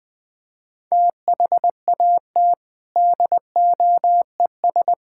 Morse code, heard as a keyed tone, about 21 wpm, 700 Hz. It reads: THAT DOES
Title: Build A Phrase - 4-Word Phrases - 20wpm